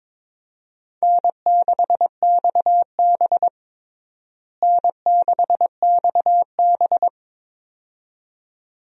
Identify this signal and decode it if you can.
Morse code, heard as a keyed tone, about 22 wpm, 700 Hz. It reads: N6XB N6XB
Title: USA Call Signs - 4 Characters - Repeated 2x - 22wpm